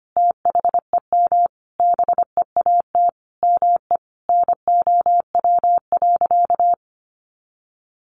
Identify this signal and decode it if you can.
Morse code, heard as a keyed tone, about 25 wpm, 700 Hz. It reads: THEM BEAT ME NOW.
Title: Sentences from Top 1300 Words 25wpm